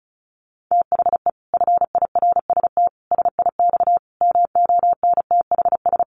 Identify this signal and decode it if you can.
Morse code, heard as a keyed tone, about 34 wpm, 700 Hz. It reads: THE FIRST SIX MONTHS
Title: Sets of 5 Words Encore2 35wpm